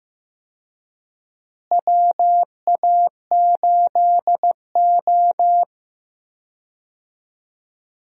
Morse code, heard as a keyed tone, about 15 wpm, 700 Hz. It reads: WA8O
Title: USA Call Signs - 4 Characters - Repeated 3x - 15wpm